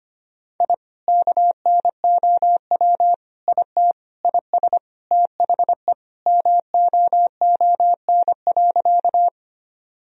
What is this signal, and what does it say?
Morse code, heard as a keyed tone, about 25 wpm, 700 Hz. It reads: I KNOW IT IS THE MOON.
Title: Sentences from Top 1000 Words - Review Part 3 - 25wpm